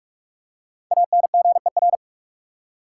Morse code, heard as a keyed tone, about 46 wpm, 700 Hz. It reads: ANGER
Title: Five Letter Words - 45wpm